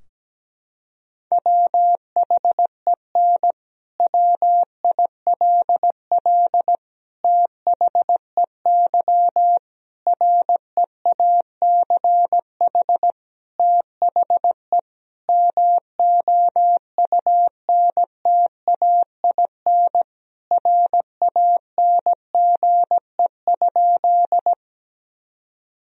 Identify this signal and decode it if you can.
Morse code, heard as a keyed tone, about 17 wpm, 700 Hz. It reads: WHEN WILL THEY REACH THE MOUNTAIN RANGE?
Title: Sentences from Top 1000 Words - Review Part 4 - 17wpm